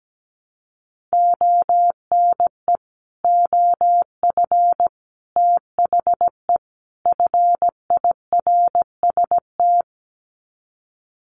Morse code, heard as a keyed tone, about 17 wpm, 700 Hz. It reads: ONE OF THE FIRST
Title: Sets of 4 Words 17wpm